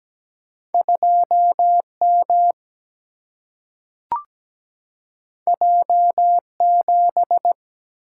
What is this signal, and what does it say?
Morse code, heard as a keyed tone, about 17 wpm, 700 Hz. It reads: 2M J7
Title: Single-Letter Single-Number 17wpm